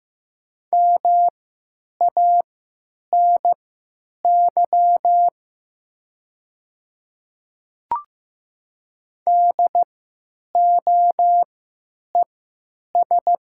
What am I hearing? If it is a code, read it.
Morse code, heard as a keyed tone, about 15 wpm, spaced as for 9 wpm, 700 Hz. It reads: MANY DOES
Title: 3x Character Spacing - Top 200 Words - 15wpm